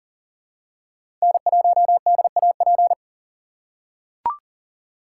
Morse code, heard as a keyed tone, about 40 wpm, 700 Hz. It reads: N1DAP
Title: USA Call Signs - 5 Characters - 3x Character Spacing - 40wpm